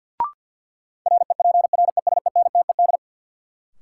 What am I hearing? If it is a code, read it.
Morse code, heard as a keyed tone, about 53 wpm, 700 Hz. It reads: REPRESENTED
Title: Sets of 5 Words - New Word Drill - 50wpm